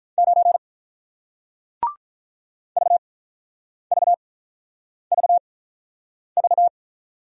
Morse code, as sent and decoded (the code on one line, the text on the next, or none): ----. / ...- ...- ...- ...-
9 VVVV